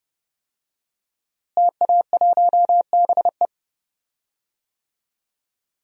Morse code, heard as a keyed tone, about 31 wpm, 700 Hz. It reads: TA1BE
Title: DXCC-100 Call Signs - 5 Characters - Repeated 3x - 30wpm